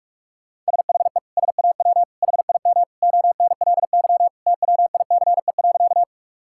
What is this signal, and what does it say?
Morse code, heard as a keyed tone, about 42 wpm, 700 Hz. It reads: SHE SAW HIM ONLY TWICE.
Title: Sentences from Top 1600 Words 45wpm